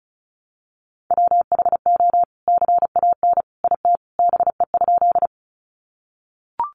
Morse code, heard as a keyed tone, about 34 wpm, 700 Hz. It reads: WHO CAN IT BE?
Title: Sentences from Top 100 Words 35wpm